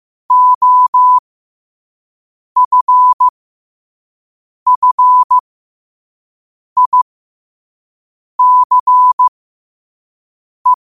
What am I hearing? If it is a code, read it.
Morse code, heard as a keyed tone, about 15 wpm, spaced as for 5 wpm, 1000 Hz. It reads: OFFICE
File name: qotdfiles / 5wpm